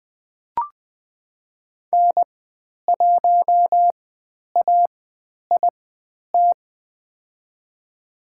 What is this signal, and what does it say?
Morse code, heard as a keyed tone, about 20 wpm, spaced as for 10 wpm, 700 Hz. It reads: N1AIT
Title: 095 Lesson 16g - Letter c - Call Signs 20wpm